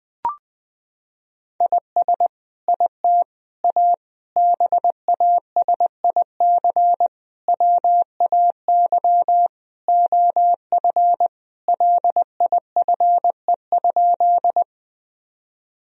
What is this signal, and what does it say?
Morse code, heard as a keyed tone, about 21 wpm, 700 Hz. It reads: IS IT A BASIC WAY OF LIFE?
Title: Sentences from Top 1000 Words - Review Part 3 - 20wpm